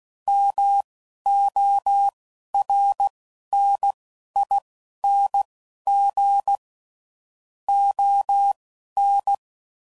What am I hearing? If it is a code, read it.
Morse code, heard as a keyed tone, about 16 wpm, spaced as for 11 wpm, 800 Hz. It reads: MORNING ON
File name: CW Code Groups @10WPM / Weather Sets / Weather 4 28